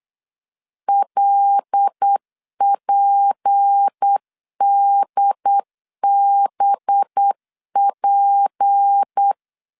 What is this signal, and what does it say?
Morse code, heard as a keyed tone, about 8 wpm, 800 Hz. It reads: LPDBP
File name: LEZIONI / 6 PKX0 / 6 P 8wpm 3025